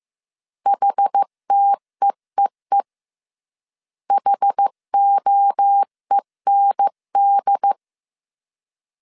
Morse code, heard as a keyed tone, about 15 wpm, 800 Hz. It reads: HTEEE HOEND
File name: CW / DET / 1 3 1500C S A 14wpm 2447